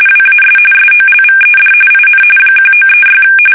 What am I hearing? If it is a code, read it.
RTTY, about 45.45 baud. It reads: WAS SOLVED IN 1936
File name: fsk850